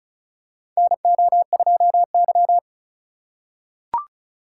Morse code, heard as a keyed tone, about 36 wpm, 700 Hz. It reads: NO2Y
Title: USA Call Signs - 4 Characters - 2x Character Spacing - 35wpm